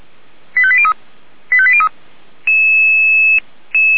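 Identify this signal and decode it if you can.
Five-tone selective call (ZVEI), heard as quick runs of five tones, tones of about 70 ms. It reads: 85803 85803
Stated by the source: Five-tone ZVEI selective call 85381, received by SDR.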